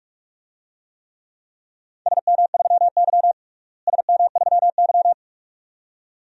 Morse code, heard as a keyed tone, about 46 wpm, 700 Hz. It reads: SM3Y SM3Y
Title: DXCC-100 Call Signs - 4 Characters - Repeated 2x - 45wpm